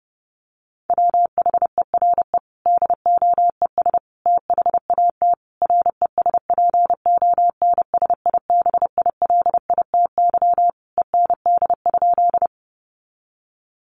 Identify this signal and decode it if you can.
Morse code, heard as a keyed tone, about 30 wpm, 700 Hz. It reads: WHERE DOES THAT RESPONSIBILITY END?
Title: Sentences from Top 1200 Words 30wpm